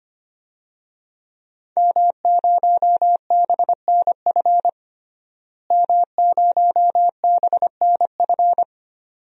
Morse code, heard as a keyed tone, about 25 wpm, 700 Hz. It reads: M0BNF M0BNF
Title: DXCC-100 Call Signs - 5 Characters - Repeated 2x - 25wpm